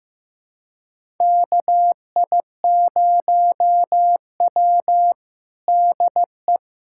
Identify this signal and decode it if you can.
Morse code, heard as a keyed tone, about 15 wpm, 700 Hz. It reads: KI0W DE
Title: USA Callsign DE Callsign - 4 Characters - 15wpm